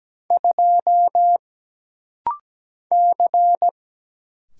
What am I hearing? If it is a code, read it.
Morse code, heard as a keyed tone, about 17 wpm, 700 Hz. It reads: 2 C
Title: Single Letter-Number - Rapid-Fire 17wpm